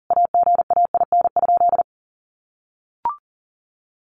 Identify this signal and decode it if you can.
Morse code, heard as a keyed tone, about 40 wpm, 700 Hz. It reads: AGAIN?
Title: A Sound Of Thunder 40wpm